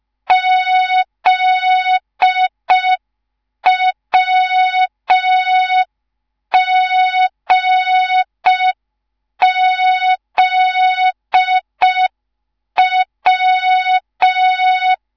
Morse code, heard as a keyed tone, about 5 wpm, 3700 Hz. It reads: ZWGZW
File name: Morze-HA2MN / hang / 10GWZ HA2MN